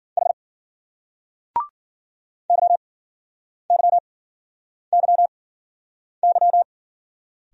Morse code, heard as a keyed tone, about 50 wpm, spaced as for 9 wpm, 700 Hz. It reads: H YYYY